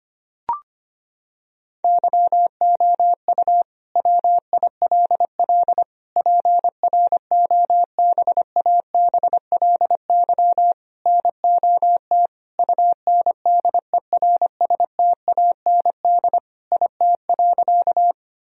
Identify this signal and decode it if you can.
Morse code, heard as a keyed tone, about 25 wpm, 700 Hz. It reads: YOU WILL PROBABLY NOT UNDERSTAND IT.
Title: Sentences from Top 1000 Words - Review Part 4 - 25wpm